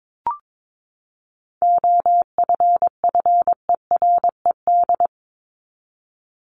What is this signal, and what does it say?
Morse code, heard as a keyed tone, about 22 wpm, 700 Hz. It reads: OFFERED
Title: ED Suffix - Top 100 Words with Base Words 22wpm